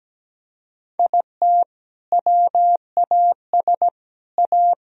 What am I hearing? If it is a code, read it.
Morse code, heard as a keyed tone, about 17 wpm, 700 Hz. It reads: IT WAS A
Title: Sentences from Top 1000 Words - Review Part 2 - 17wpm